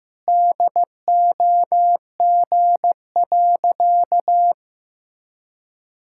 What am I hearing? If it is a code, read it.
Morse code, heard as a keyed tone, about 16 wpm, 700 Hz. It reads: DOG.
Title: Pangram Challenge v2 - 15wpm